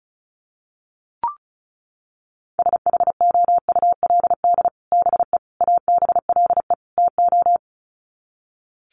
Morse code, heard as a keyed tone, about 34 wpm, 700 Hz. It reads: SHOULD BE ABLE TO
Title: Sets of 4 Words 35wpm